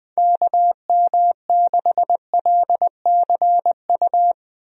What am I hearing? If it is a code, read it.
Morse code, heard as a keyed tone, about 21 wpm, 700 Hz. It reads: KM6LCU
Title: USA Call Signs - 6 Characters - Repeated 3x - 20wpm